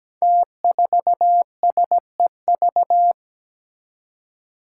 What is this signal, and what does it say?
Morse code, heard as a keyed tone, about 17 wpm, 700 Hz. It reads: T4SEV
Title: Five Challenging Letter-Number Combinations 17wpm